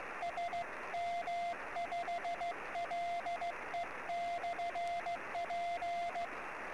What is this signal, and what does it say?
Morse code, heard as a keyed tone, about 16 wpm, 690 Hz, with radio noise. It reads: SM5LE/P
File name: sm7lcb / 08 sm5le-p jo98ax 23cm